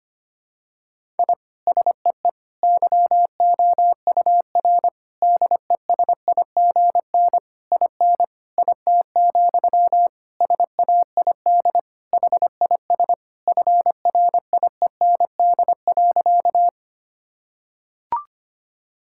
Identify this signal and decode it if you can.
Morse code, heard as a keyed tone, about 25 wpm, 700 Hz. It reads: I SEE YOUR DESIGN IN IT, SAID HIS FRIEND.
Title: Sentences from Top 1000 Words - Review Part 5 - 25wpm